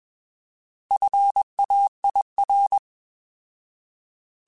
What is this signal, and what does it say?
Morse code, heard as a keyed tone, about 21 wpm, 800 Hz. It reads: FAIR
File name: Alphabetical / Alphabetical mp3s / Fs @20